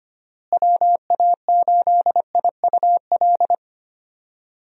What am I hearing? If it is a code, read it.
Morse code, heard as a keyed tone, about 25 wpm, 700 Hz. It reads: WA8IUL
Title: USA Call Signs - 6 Characters - Repeated 3x - 25wpm